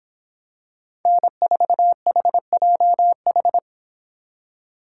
Morse code, heard as a keyed tone, about 26 wpm, 700 Hz. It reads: N4HJH